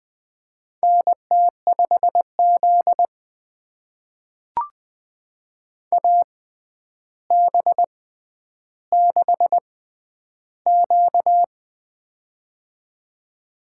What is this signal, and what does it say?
Morse code, heard as a keyed tone, about 20 wpm, spaced as for 10 wpm, 700 Hz. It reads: NT5Z AB6Q
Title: USA Call Signs - 4 Characters - 6x Character Spacing - 20wpm